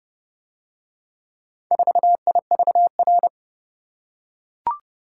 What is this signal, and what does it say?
Morse code, heard as a keyed tone, about 28 wpm, 700 Hz. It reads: 4IVR